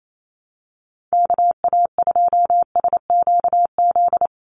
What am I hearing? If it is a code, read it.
Morse code, heard as a keyed tone, about 28 wpm, 700 Hz. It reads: KA2SQZ
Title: USA Call Signs Repeated 2x 28wpm